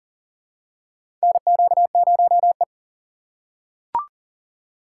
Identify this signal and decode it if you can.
Morse code, heard as a keyed tone, about 40 wpm, 700 Hz. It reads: NQ0E
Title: USA Call Signs - 4 Characters - 4x Character Spacing - 40wpm